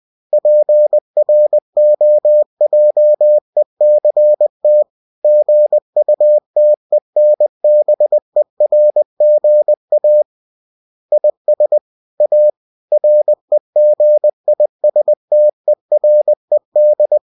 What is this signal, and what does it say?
Morse code, heard as a keyed tone, about 20 wpm, 600 Hz. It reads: PROJECT GUTENBERGA IS A REGISTERED
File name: cw-books / winnie-the-pooh / winnie-20wpm-page-0113